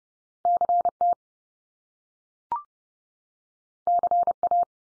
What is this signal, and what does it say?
Morse code, heard as a keyed tone, about 30 wpm, 700 Hz. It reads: CT CA
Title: US States 30WPM